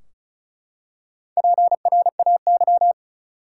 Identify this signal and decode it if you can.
Morse code, heard as a keyed tone, about 34 wpm, 700 Hz. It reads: PRAY
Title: Four Letter Words - 35wpm